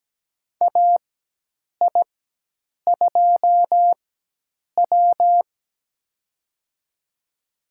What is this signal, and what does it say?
Morse code, heard as a keyed tone, about 17 wpm, spaced as for 8 wpm, 700 Hz. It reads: AI2W
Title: USA Call Signs - 4 Characters - 4x Character Spacing - 17wpm